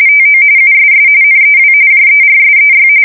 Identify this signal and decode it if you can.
RTTY, about 45.45 baud. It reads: TO USS NEW JERSEY
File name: fsk170 / battleshipnj